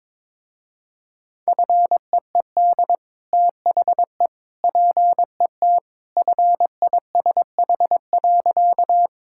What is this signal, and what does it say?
Morse code, heard as a keyed tone, about 22 wpm, 700 Hz. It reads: FEED THE PET FISH.
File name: practice / 22wpm / Easy Sentences with Hints - 22wpm